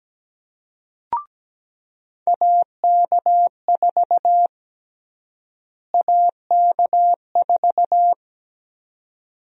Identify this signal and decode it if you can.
Morse code, heard as a keyed tone, about 17 wpm, 700 Hz. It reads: AK4 AK4
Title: USA Call Sign Prefixes - Repeated 2x - 17wpm